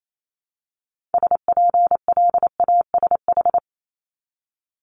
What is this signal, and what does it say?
Morse code, heard as a keyed tone, about 28 wpm, 700 Hz. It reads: SPLASH
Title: Multi-letter Phonemes Repeated 3x 28wpm